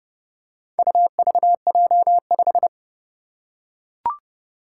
Morse code, heard as a keyed tone, about 31 wpm, 700 Hz. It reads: UVJ5
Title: Four Challenging Letter-Number Combinations 30wpm